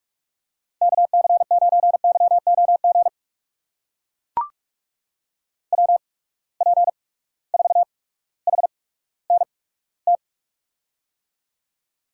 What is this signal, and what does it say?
Morse code, heard as a keyed tone, about 42 wpm, spaced as for 19 wpm, 700 Hz. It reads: KC9YOG WP4HNT
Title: USA Call Signs - 6 Characters - 8x Character Spacing - 45wpm